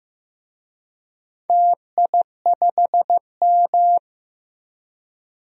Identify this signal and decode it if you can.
Morse code, heard as a keyed tone, about 15 wpm, 700 Hz. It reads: TI5M
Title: DXCC-100 Call Signs - 4 Characters - 15wpm